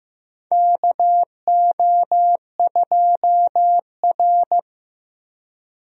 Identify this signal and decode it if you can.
Morse code, heard as a keyed tone, about 15 wpm, 700 Hz. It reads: KO2R
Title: USA Call Signs - 4 Characters - Repeated 3x - 15wpm